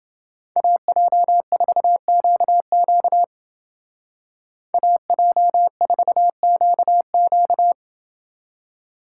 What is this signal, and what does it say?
Morse code, heard as a keyed tone, about 29 wpm, 700 Hz. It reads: AJ4QQ AJ4QQ